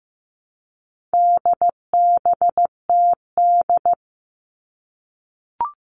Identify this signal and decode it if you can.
Morse code, heard as a keyed tone, about 15 wpm, 700 Hz. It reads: DBTD
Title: All Cut Numbers - 4-Digit 15wpm